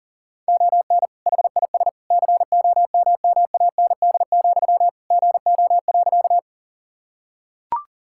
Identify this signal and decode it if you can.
Morse code, heard as a keyed tone, about 40 wpm, 700 Hz. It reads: ON HIS COMMAND, GO.
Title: Sentences from Top 1600 Words 40wpm